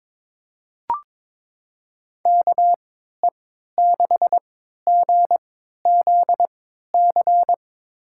Morse code, heard as a keyed tone, about 22 wpm, spaced as for 12 wpm, 700 Hz. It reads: KE6GZC
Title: USA Call Signs - 6 Characters - 3x Character Spacing - 22wpm